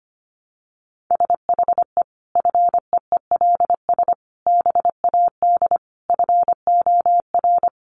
Morse code, heard as a keyed tone, about 25 wpm, 700 Hz. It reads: SHE FEELS BAD FOR
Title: Additional Easy Phrases 25wpm